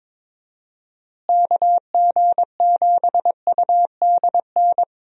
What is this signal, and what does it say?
Morse code, heard as a keyed tone, about 22 wpm, 700 Hz. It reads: KG7UDN
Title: USA Call Signs - 6 Characters - Repeated 3x - 22wpm